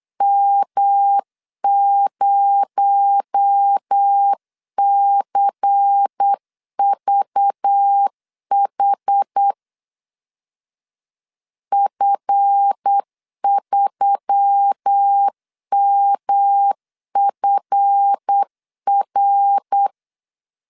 Morse code, heard as a keyed tone, about 8 wpm, 800 Hz. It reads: M0CVH F3MFR
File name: LEZIONI / 7 GF9Z / 7 GF 8wpm 3108